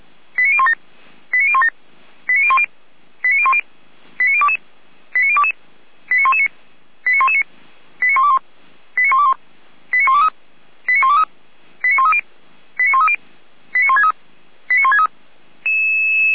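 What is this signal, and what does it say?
Five-tone selective call (ZVEI), heard as quick runs of five tones, tones of about 70 ms. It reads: 89017 89017 89010 89010 89022 89022 89119 89119 89121 89121 89123 89123 89130 89130 89163 89163
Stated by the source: ZVEI five-tone selective call received by SDR.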